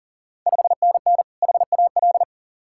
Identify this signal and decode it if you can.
Morse code, heard as a keyed tone, about 40 wpm, 690 Hz. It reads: 5NN HAL
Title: Ontario QSO Party Exchange 40wpm